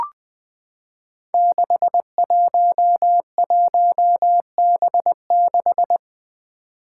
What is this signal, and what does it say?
Morse code, heard as a keyed tone, about 20 wpm, 700 Hz. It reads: 611B6
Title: Five Challenging Letter-Number Combinations 20wpm